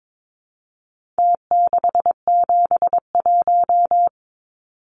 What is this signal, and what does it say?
Morse code, heard as a keyed tone, about 22 wpm, 700 Hz. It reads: T671
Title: Numbers - Contest Style - 4-Digit 22wpm